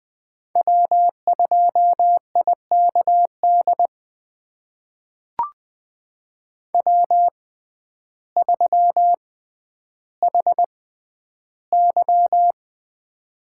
Morse code, heard as a keyed tone, about 20 wpm, spaced as for 11 wpm, 700 Hz. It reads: W2IKD W3HY
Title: USA Call Signs - 6x Character Spacing - 20wpm